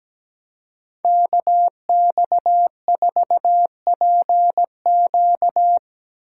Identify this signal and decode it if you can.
Morse code, heard as a keyed tone, about 17 wpm, 700 Hz. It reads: KX4PQ
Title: K1USN SST QSO Practice - 17wpm